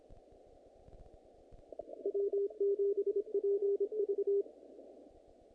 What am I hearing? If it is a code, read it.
Morse code, heard as a keyed tone, about 26 wpm, 410 Hz. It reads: W7PV
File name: contests / naqp-cw-k5tr-op-k5pi- -1900